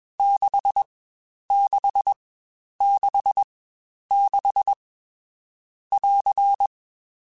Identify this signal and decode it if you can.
Morse code, heard as a keyed tone, about 21 wpm, spaced as for 10 wpm, 800 Hz. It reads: 6666 <AR>
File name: Short Runs / Short Runs-mp3s / Staccato@20